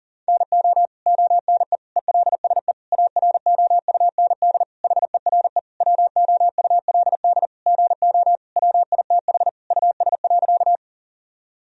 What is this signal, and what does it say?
Morse code, heard as a keyed tone, about 40 wpm, 700 Hz. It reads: NO ONE ELSE AROUND HERE WOULD GO WITH US.
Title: Sentences from Top 1000 Words - Review Part 3 - 40wpm